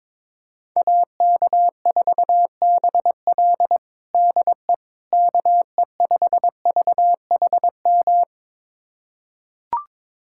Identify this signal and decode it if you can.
Morse code, heard as a keyed tone, about 21 wpm, 700 Hz. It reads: AK4BL DE KE5VHM
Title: USA Callsign DE Callsign 22wpm